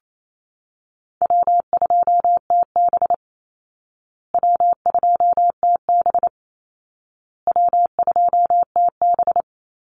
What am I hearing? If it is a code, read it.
Morse code, heard as a keyed tone, about 28 wpm, 700 Hz. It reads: W2TB W2TB W2TB
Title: Easy Call Signs Repeated 3x - 28wpm